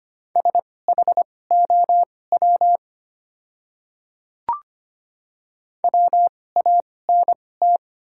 Morse code, heard as a keyed tone, about 25 wpm, spaced as for 18 wpm, 700 Hz. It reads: SHOW WANT
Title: 2x Character Spacing - Top 200 Words - 25wpm